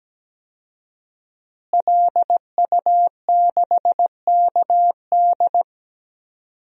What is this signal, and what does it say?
Morse code, heard as a keyed tone, about 17 wpm, 700 Hz. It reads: LU6KD
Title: DXCC-100 Call Signs - 5 Characters - Repeated 2x - 17wpm